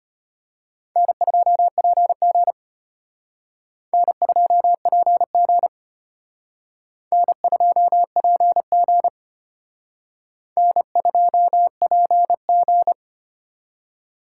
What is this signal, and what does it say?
Morse code, heard as a keyed tone, about 31 wpm, 700 Hz. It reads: N2PG N2PG N2PG N2PG